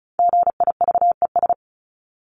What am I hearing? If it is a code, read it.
Morse code, heard as a keyed tone, about 34 wpm, 700 Hz. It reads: GIVES
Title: ES:S Suffix - Top 100 Words with Base Words 35wpm